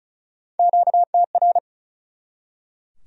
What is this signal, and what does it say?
Morse code, heard as a keyed tone, about 48 wpm, spaced as for 42 wpm, 700 Hz. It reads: QTR
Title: Speed-Racing Q-Codes - 35wpm